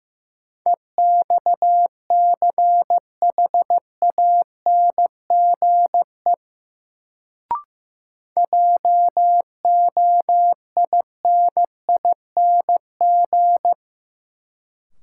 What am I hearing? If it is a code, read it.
Morse code, heard as a keyed tone, about 15 wpm, 700 Hz. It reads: EXCHANGE JOINING
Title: Sets of 5 Words - New Word Drill - 15wpm